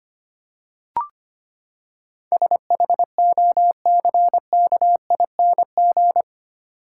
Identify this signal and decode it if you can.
Morse code, heard as a keyed tone, about 25 wpm, 700 Hz. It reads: SHOCKING
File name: Eight Letter Words - 25wpm